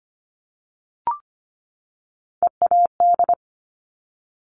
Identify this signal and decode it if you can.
Morse code, heard as a keyed tone, about 25 wpm, 700 Hz. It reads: EAD
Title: Top Three Letter Combinations in Words 25wpm